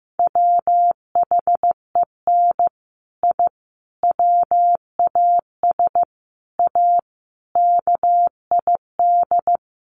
Morse code, heard as a keyed tone, about 15 wpm, 700 Hz. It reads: WHEN I WAS A KID
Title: Sets of 5 Words 15wpm